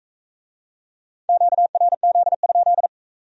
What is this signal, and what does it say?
Morse code, heard as a keyed tone, about 44 wpm, 700 Hz. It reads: QRZ?